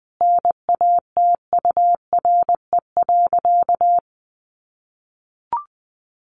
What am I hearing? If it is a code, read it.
Morse code, heard as a keyed tone, about 20 wpm, 700 Hz. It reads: NATURE.
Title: Sentences from Top 1200 Words 20wpm